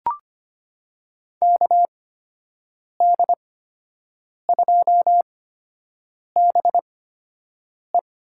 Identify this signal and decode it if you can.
Morse code, heard as a keyed tone, about 25 wpm, spaced as for 7 wpm, 700 Hz. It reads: KD2BE
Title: USA Call Signs - 5 Characters - 8x Character Spacing - 25wpm